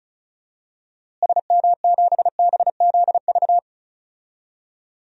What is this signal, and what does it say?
Morse code, heard as a keyed tone, about 36 wpm, 700 Hz. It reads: SM7BZV
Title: DXCC-100 Call Signs - 6 Characters - Repeated 3x - 35wpm